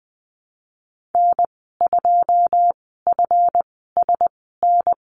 Morse code, heard as a keyed tone, about 20 wpm, spaced as for 15 wpm, 700 Hz. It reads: N2FSN
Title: 147 Lesson 24g - Letter y - Call Signs 20wpm